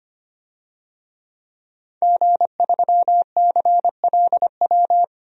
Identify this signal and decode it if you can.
Morse code, heard as a keyed tone, about 25 wpm, 700 Hz. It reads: G3CLW
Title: DXCC-100 Call Signs - 5 Characters - Repeated 3x - 25wpm